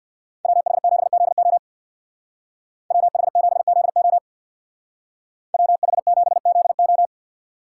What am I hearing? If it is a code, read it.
Morse code, heard as a keyed tone, about 62 wpm, 700 Hz. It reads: WH7ZO WH7ZO WH7ZO
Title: Speed Racing - USA 2x2 Call Signs 50wpm